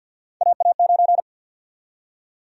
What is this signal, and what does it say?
Morse code, heard as a keyed tone, about 51 wpm, 700 Hz. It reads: AA9
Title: USA Call Sign Prefixes - 50wpm